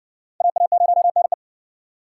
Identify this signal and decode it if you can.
Morse code, heard as a keyed tone, about 60 wpm, 700 Hz. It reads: AA0NE